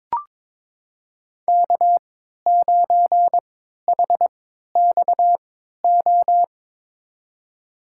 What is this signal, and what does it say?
Morse code, heard as a keyed tone, about 22 wpm, spaced as for 13 wpm, 700 Hz. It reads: K9HXO
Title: USA Call Signs - 5 Characters - 3x Character Spacing - 22wpm